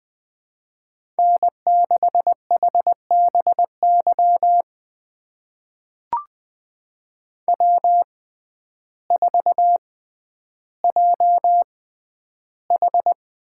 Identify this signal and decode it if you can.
Morse code, heard as a keyed tone, about 20 wpm, spaced as for 11 wpm, 700 Hz. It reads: N6HBY W4JH
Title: USA Call Signs - 5 Characters - 6x Character Spacing - 20wpm